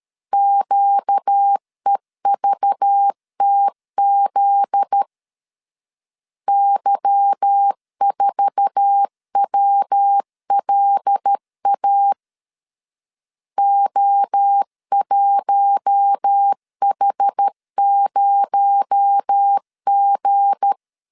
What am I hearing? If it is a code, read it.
Morse code, heard as a keyed tone, about 13 wpm, 800 Hz. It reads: QEVTZ Y4WLA O1H0G
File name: LEZIONI / 8 YJ45 / 8 YJ4 12wpm 2151